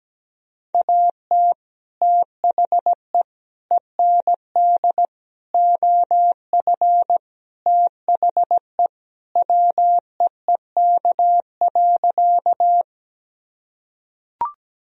Morse code, heard as a keyed tone, about 17 wpm, 700 Hz. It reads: AT THE END OF THE WEEK.
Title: Sentences from Top 1000 Words - Review Part 2 - 17wpm